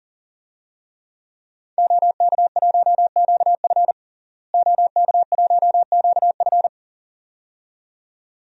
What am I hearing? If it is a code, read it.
Morse code, heard as a keyed tone, about 40 wpm, 700 Hz. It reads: OK1QF OK1QF
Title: DXCC-100 Call Signs - 5 Characters - Repeated 2x - 40wpm